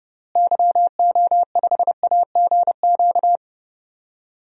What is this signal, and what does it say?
Morse code, heard as a keyed tone, about 30 wpm, 700 Hz. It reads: YO5AGQ
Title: DXCC-100 Call Signs - 6 Characters - Repeated 2x - 30wpm